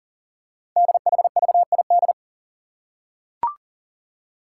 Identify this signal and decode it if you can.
Morse code, heard as a keyed tone, about 40 wpm, 700 Hz. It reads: DHVID
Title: Code Groups - 5 Letters - 40wpm